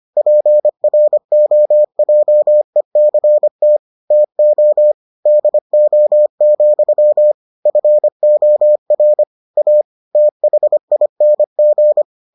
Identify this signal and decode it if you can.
Morse code, heard as a keyed tone, about 25 wpm, 600 Hz. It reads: PROJECT TO DO, FOR A THING